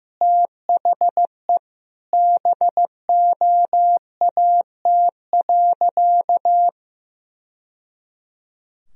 Morse code, heard as a keyed tone, about 15 wpm, 700 Hz. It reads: THE BOAT.
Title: Sentences from Top 1000 Words - Review Part 3 - 15wpm